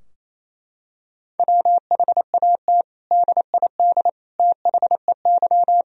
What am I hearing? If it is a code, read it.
Morse code, heard as a keyed tone, about 28 wpm, 700 Hz. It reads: WHAT DID THEY
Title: Sentences from Top 1000 Words - Review Part 4 - 28wpm